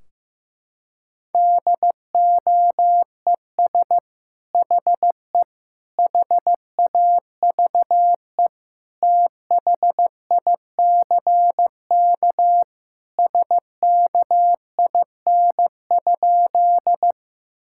Morse code, heard as a keyed tone, about 15 wpm, 700 Hz. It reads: DOES HE HAVE THICK SKIN?
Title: Sentences from Top 1600 Words 15wpm